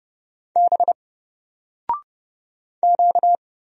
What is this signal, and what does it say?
Morse code, heard as a keyed tone, about 30 wpm, 700 Hz. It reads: B Q
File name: Single-Letter-Number-30wpm